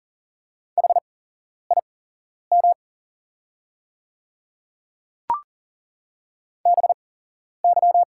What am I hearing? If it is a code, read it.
Morse code, heard as a keyed tone, about 38 wpm, spaced as for 11 wpm, 690 Hz. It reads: HIM BY
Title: 8x Character Spacing - Top 200 Words - 40wpm